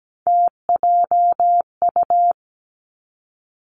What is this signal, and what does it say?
Morse code, heard as a keyed tone, about 17 wpm, 700 Hz. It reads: TJU
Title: Three Challenging Letter-Number Combinations 17wpm